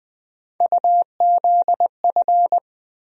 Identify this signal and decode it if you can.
Morse code, heard as a keyed tone, about 20 wpm, 700 Hz. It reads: UZF
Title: USA Call Sign Postfixes - 3 Letters 20wpm